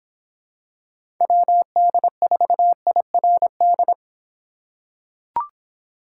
Morse code, heard as a keyed tone, about 26 wpm, 700 Hz. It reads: WD4IRD